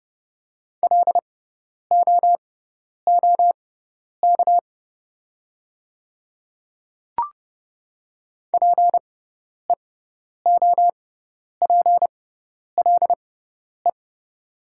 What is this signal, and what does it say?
Morse code, heard as a keyed tone, about 30 wpm, spaced as for 10 wpm, 700 Hz. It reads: LOOK PEOPLE
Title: 6x Character Spacing - Top 100 Words - 30wpm